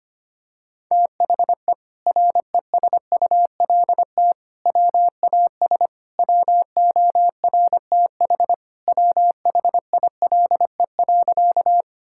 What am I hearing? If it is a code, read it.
Morse code, heard as a keyed tone, about 25 wpm, 700 Hz. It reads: THE RESULT WAS WORTH WHILE.
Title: Sentences from Top 1000 Words - Review Part 5 - 25wpm